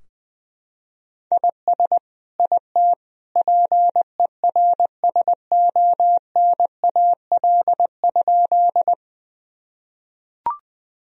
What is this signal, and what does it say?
Morse code, heard as a keyed tone, about 20 wpm, 700 Hz. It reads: IS IT PERSONAL?
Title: Sentences from Top 1000 Words - Review Part 4 - 20wpm